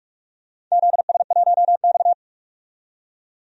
Morse code, heard as a keyed tone, about 47 wpm, 700 Hz. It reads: ZS1X
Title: DXCC-100 Call Signs - 4 Characters - 45wpm